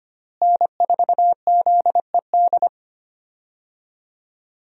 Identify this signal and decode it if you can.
Morse code, heard as a keyed tone, about 25 wpm, 700 Hz. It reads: N4ZED
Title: USA Call Signs - 5 Characters - Repeated 3x - 25wpm